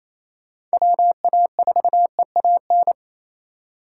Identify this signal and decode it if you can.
Morse code, heard as a keyed tone, about 37 wpm, 700 Hz. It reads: WA4EAN